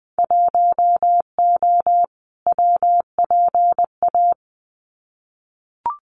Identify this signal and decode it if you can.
Morse code, heard as a keyed tone, about 20 wpm, 700 Hz. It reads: 1O WPA
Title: Winter Field Day Exchanges 20wpm